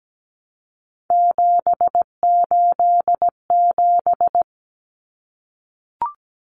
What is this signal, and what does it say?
Morse code, heard as a keyed tone, about 17 wpm, 700 Hz. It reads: 787
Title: Common Cut Numbers - 3-Digit 17wpm